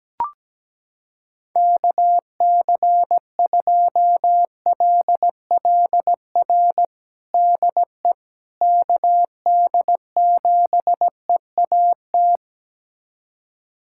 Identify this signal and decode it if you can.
Morse code, heard as a keyed tone, about 17 wpm, 700 Hz. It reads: KC2LLR DE KD7EAT
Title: USA Callsign DE Callsign - 6 Characters - 17wpm